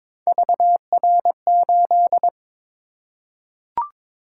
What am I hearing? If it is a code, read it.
Morse code, heard as a keyed tone, about 23 wpm, 700 Hz. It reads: VR8
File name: practice / 22wpm / Three Letter-Number Combinations - 22wpm